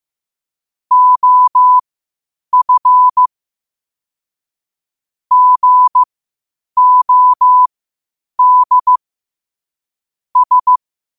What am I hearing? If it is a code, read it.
Morse code, heard as a keyed tone, about 15 wpm, spaced as for 8 wpm, 1000 Hz. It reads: OF GOD S